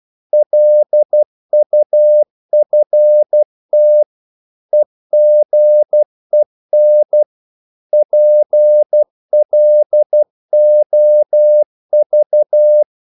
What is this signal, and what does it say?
Morse code, heard as a keyed tone, about 12 wpm, 600 Hz. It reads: LUFT EGEN PLOV
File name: Korte ord DK 12wpm